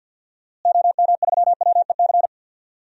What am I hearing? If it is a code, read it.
Morse code, heard as a keyed tone, about 53 wpm, 700 Hz. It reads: OM3WEC
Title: DXCC-100 Call Signs - 6 Characters - Repeated 2x - 50wpm